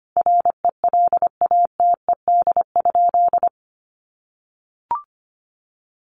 Morse code, heard as a keyed tone, about 25 wpm, 700 Hz. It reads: RELATED?
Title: Sentences from Top 1400 Words 25wpm